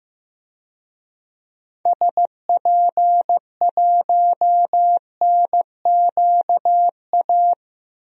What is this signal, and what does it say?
Morse code, heard as a keyed tone, about 15 wpm, 700 Hz. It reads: SP1NQA
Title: DXCC-100 Call Signs - 6 Characters - Repeated 3x - 15wpm